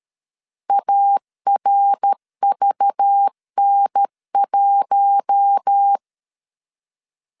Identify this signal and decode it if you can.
Morse code, heard as a keyed tone, about 13 wpm, 800 Hz. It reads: ARVN1
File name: CW / DET / 1 6 1500C S B 12wpm 3431